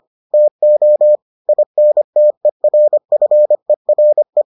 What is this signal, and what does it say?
Morse code, heard as a keyed tone, about 25 wpm, 600 Hz. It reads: TO INTERFERE